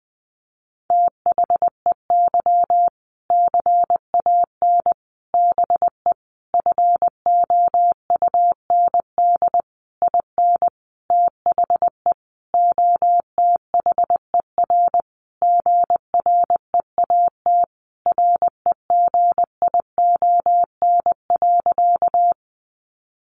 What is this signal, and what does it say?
Morse code, heard as a keyed tone, about 20 wpm, 700 Hz. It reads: THEY CAN BE FOUND IN THE OTHER GREAT REGION.
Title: Sentences from Top 1500 Words 20wpm